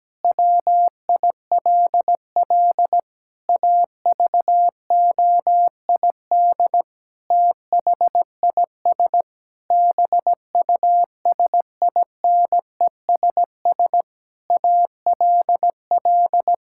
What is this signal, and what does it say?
Morse code, heard as a keyed tone, about 17 wpm, 700 Hz. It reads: WILL AVOID THIS BUSINESS ALL
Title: Sentences from Top 1000 Words - Review Part 6 - 17wpm